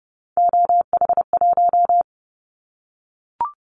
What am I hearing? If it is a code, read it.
Morse code, heard as a keyed tone, about 30 wpm, 700 Hz. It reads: OH1
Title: Callsign Prefixes for Top 50 DXCC - 30wpm